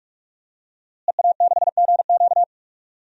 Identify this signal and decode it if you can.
Morse code, heard as a keyed tone, about 41 wpm, 700 Hz. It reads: EA6GQ
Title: DXCC-100 Call Signs - 5 Characters - Repeated 2x - 45wpm